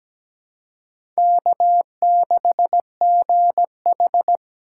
Morse code, heard as a keyed tone, about 17 wpm, 700 Hz. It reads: K6GH
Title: USA Call Signs - 4x Character Spacing - 17wpm